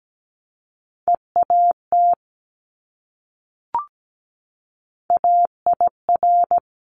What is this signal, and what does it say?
Morse code, heard as a keyed tone, about 17 wpm, 700 Hz. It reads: EAT AIR
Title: Multi-letter Phonemes Repeated 3x 17wpm